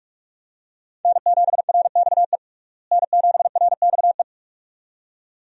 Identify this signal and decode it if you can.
Morse code, heard as a keyed tone, about 46 wpm, 700 Hz. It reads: N7RXE N7RXE
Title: USA Call Signs - 5 Characters - Repeated 2x - 45wpm